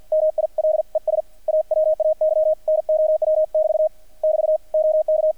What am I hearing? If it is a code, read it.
Morse code, heard as a keyed tone, about 50 wpm, 620 Hz. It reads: MILES AWAY NOW <BT> <BT> OK
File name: archived / CRSnet